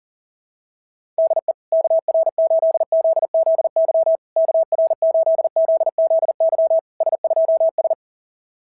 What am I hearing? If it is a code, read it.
Morse code, heard as a keyed tone, about 40 wpm, 690 Hz. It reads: DE KR8ZZY KR8ZZY S2S
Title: SOTA QSO Practice 40wpm